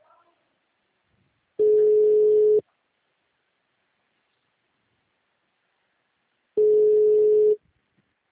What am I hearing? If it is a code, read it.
Ringback tone (Europe, 425 Hz).
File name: out- -2016- - - .20